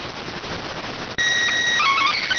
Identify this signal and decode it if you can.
SSTV (slow-scan television), PD 290.